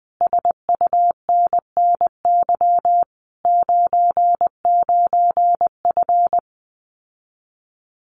Morse code, heard as a keyed tone, about 20 wpm, 700 Hz. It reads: SUNNY 99F
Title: Common QSO Phrases Encore 20wpm